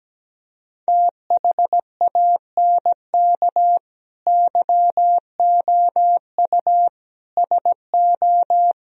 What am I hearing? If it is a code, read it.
Morse code, heard as a keyed tone, about 17 wpm, 700 Hz. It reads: THANK YOU SO
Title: Speed-Racing - Sets of 4 Words 17wpm